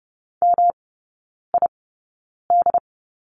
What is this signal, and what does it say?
Morse code, heard as a keyed tone, about 30 wpm, spaced as for 9 wpm, 700 Hz. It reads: MID
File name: Lesson 19e - Letter w - Three Characters 30wpm